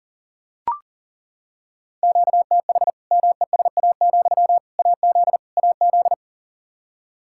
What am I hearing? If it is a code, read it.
Morse code, heard as a keyed tone, about 40 wpm, 700 Hz. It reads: QTH MESA, AZ AZ
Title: Practice QSO - Exchange 1 - Beginning - 40wpm